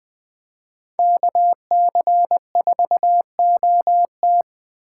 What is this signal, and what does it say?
Morse code, heard as a keyed tone, about 23 wpm, 700 Hz. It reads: KC4OT
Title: Speed Racing - USA 2x2 Call Signs 20wpm